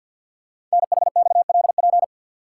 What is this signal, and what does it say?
Morse code, heard as a keyed tone, about 47 wpm, 700 Hz. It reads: NHXLP
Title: Code Groups - 5 Letter:Numbers - 50wpm